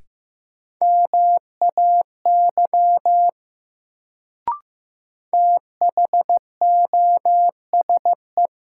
Morse code, heard as a keyed tone, about 15 wpm, 700 Hz. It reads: MAY THOSE
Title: Sets of 2 Words - New Word Drill - 15wpm